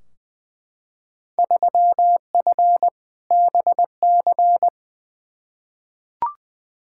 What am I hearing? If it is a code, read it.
Morse code, heard as a keyed tone, about 20 wpm, 700 Hz. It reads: 3F BC
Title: Field Day Exchanges Encore - 20wpm